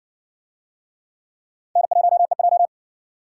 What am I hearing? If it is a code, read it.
Morse code, heard as a keyed tone, about 58 wpm, 700 Hz. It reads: N1EJ